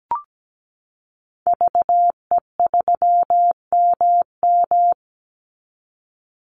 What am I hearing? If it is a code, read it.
Morse code, heard as a keyed tone, about 17 wpm, 700 Hz. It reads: VE3MM
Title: Call Signs Repeated 2x - 17wpm